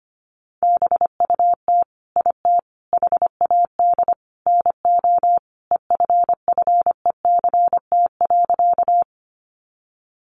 Morse code, heard as a keyed tone, about 25 wpm, 700 Hz. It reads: BUT IT HAD NO EFFECT.
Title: Sentences from Top 1000 Words 25wpm